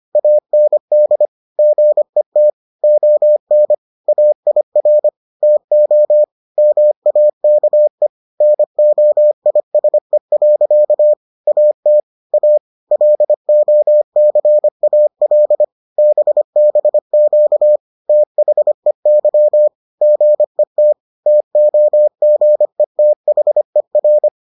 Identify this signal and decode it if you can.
Morse code, heard as a keyed tone, about 25 wpm, 600 Hz. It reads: AND GET ON AIR TO MAKE NOISE. AT A LOCAL BBQ THEY GET TOGETHER